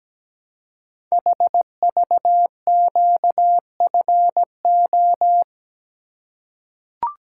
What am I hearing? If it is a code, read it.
Morse code, heard as a keyed tone, about 17 wpm, 700 Hz. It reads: HVQFO
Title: Code Groups - 5 Letters - 17wpm